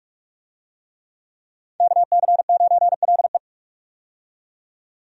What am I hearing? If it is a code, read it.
Morse code, heard as a keyed tone, about 46 wpm, 700 Hz. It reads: KC9LE
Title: USA Call Signs - 5 Characters - 45wpm